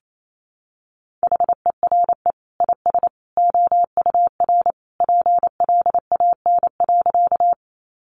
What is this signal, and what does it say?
Morse code, heard as a keyed tone, about 28 wpm, 700 Hz. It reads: HERE IS OUR PLAN.
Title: Sentences from Top 500 Words 28wpm